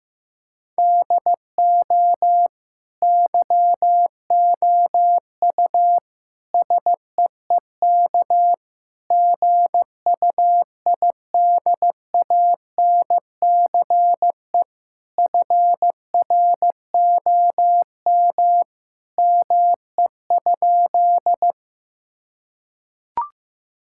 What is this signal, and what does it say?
Morse code, heard as a keyed tone, about 15 wpm, 700 Hz. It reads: DO YOU SEEK GUIDANCE FROM ME?
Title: Sentences from Top 1600 Words 15wpm